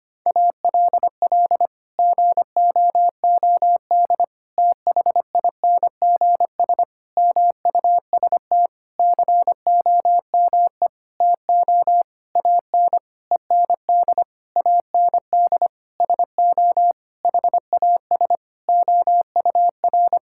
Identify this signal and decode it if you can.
Morse code, heard as a keyed tone, about 25 wpm, 700 Hz. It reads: ALL GOOD THINGS MUST COME TO AN END AND SO HAS OUR
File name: practice / 25wpm / Sentences with 9-Word Phrases - 25wpm